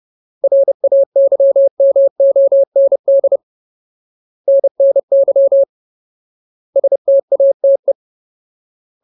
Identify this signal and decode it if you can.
Morse code, heard as a keyed tone, about 30 wpm, 550 Hz. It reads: RAYMOND NNY STATE
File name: Ultimate Word List / 30WPM Ultimate